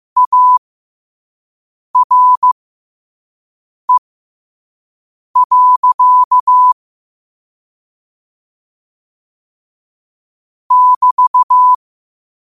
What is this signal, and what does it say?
Morse code, heard as a keyed tone, about 15 wpm, spaced as for 5 wpm, 1000 Hz. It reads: ARE. <BT>
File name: qotdfiles / 5wpm